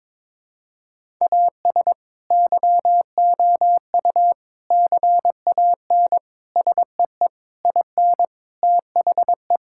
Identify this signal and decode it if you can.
Morse code, heard as a keyed tone, about 22 wpm, 700 Hz. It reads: AS YOU CAN SEE IN THE
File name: practice / 22wpm / Sets of 7 Words - 22wpm